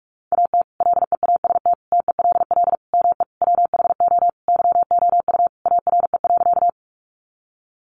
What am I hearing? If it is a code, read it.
Morse code, heard as a keyed tone, about 44 wpm, 700 Hz. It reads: AT LEAST TELL ME WHO YOU ARE.
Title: Sentences from Top 700 Words 45wpm